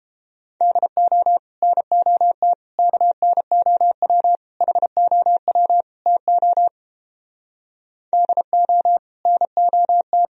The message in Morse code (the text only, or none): DO NOT KNOW HOW TO DO NOT